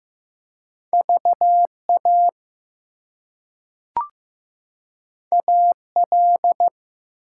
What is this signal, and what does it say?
Morse code, heard as a keyed tone, about 15 wpm, 700 Hz. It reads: VA AL
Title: US States and Canadian Provinces 15wpm